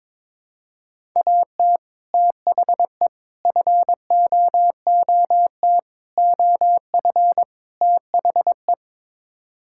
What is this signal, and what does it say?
Morse code, heard as a keyed tone, about 21 wpm, 700 Hz. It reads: AT THE FOOT OF THE
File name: practice / 22wpm / Build A Phrase - 5-Word Phrases - 22wpm